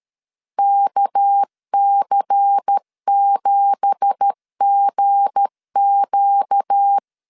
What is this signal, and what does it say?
Morse code, heard as a keyed tone, about 13 wpm, 800 Hz. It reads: KC7GQ
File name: CW / DET / 1 9 1500C S C 12wpm 3811